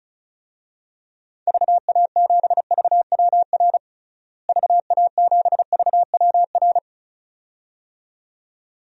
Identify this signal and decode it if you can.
Morse code, heard as a keyed tone, about 36 wpm, 700 Hz. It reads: VA7VWR VA7VWR
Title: DXCC-100 Call Signs - 6 Characters - Repeated 2x - 35wpm